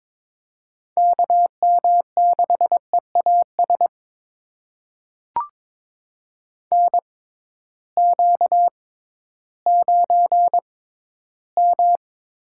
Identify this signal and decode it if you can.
Morse code, heard as a keyed tone, about 22 wpm, spaced as for 13 wpm, 700 Hz. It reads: KM6EAS NQ9M
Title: USA Call Signs - 6x Character Spacing - 22wpm